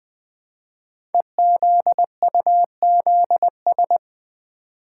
Morse code, heard as a keyed tone, about 20 wpm, 700 Hz. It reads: EZUZS
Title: Code Groups - 5 Letters - 20wpm